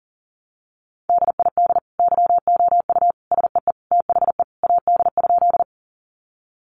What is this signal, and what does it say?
Morse code, heard as a keyed tone, about 37 wpm, 700 Hz. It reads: DID YOU SEE THE AD?
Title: Sentences from Top 1300 Words 40wpm